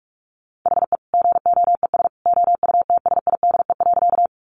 Morse code, heard as a keyed tone, about 44 wpm, 700 Hz. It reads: HE GOES OUTSIDE.
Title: Easy Phrases 45wpm